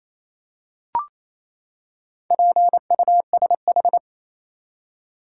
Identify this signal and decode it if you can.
Morse code, heard as a keyed tone, about 28 wpm, 700 Hz. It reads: PUSH
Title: Four Letter Words - 28wpm